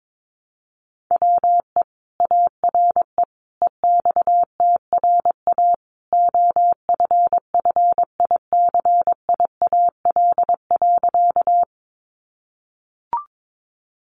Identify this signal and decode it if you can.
Morse code, heard as a keyed tone, about 22 wpm, 700 Hz. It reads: WE ARE EXTRA OFFICIAL.
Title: Sentences from Top 1000 Words 22wpm